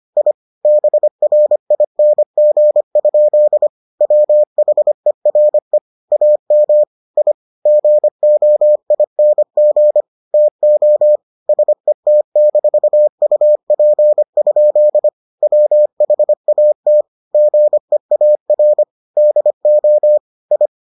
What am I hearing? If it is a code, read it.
Morse code, heard as a keyed tone, about 25 wpm, 600 Hz. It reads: I BRING? WHERE AM I GOING TO SET-UP? WHAT GEAR DO I